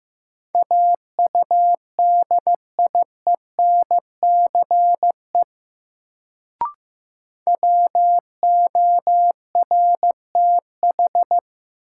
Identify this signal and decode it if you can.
Morse code, heard as a keyed tone, about 15 wpm, 700 Hz. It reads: AUDIENCE WORTH
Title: Sentences from Top 800 Words - New Word Drill - 15wpm